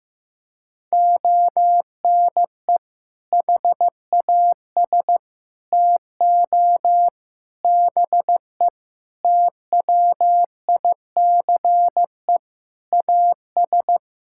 Morse code, heard as a keyed tone, about 16 wpm, 700 Hz. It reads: ONE HAS TO BE TWICE AS
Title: Sentences from Top 1600 Words 15wpm